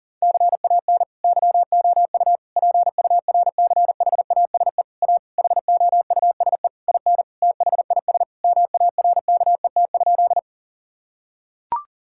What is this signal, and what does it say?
Morse code, heard as a keyed tone, about 40 wpm, 700 Hz. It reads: CAN YOU PURCHASE A HOUSE IN THIS MARKET?
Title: Sentences from Top 1000 Words - Review Part 5 - 40wpm